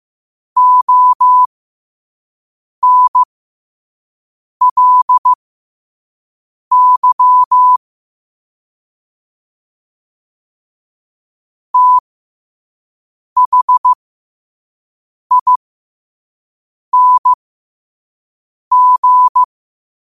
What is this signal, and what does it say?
Morse code, heard as a keyed tone, about 15 wpm, spaced as for 5 wpm, 1000 Hz. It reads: ONLY THING
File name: qotdfiles / 5wpm